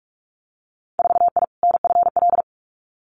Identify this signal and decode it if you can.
Morse code, heard as a keyed tone, about 46 wpm, 700 Hz. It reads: 4I NFL
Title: Winter Field Day Exchanges 45wpm